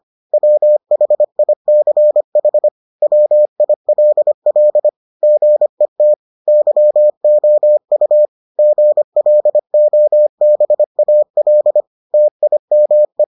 Morse code, heard as a keyed tone, about 25 wpm, 600 Hz. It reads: WHICH WILL GET YOU GLOBAL TIME